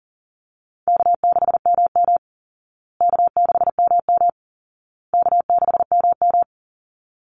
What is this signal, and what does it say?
Morse code, heard as a keyed tone, about 40 wpm, 700 Hz. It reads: K6MM K6MM K6MM
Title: Easy Call Signs Repeated 3x - 40wpm